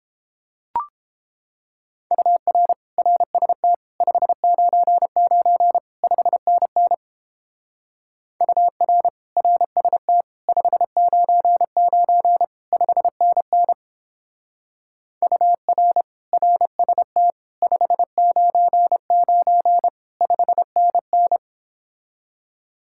Morse code, heard as a keyed tone, about 29 wpm, 700 Hz. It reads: UR RST 599 5NN UR RST 599 5NN UR RST 599 5NN